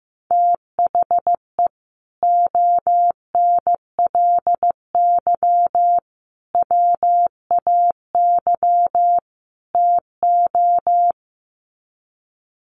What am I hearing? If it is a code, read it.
Morse code, heard as a keyed tone, about 15 wpm, 700 Hz. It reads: THE ONLY WAY TO
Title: Sets of 4 Words 15wpm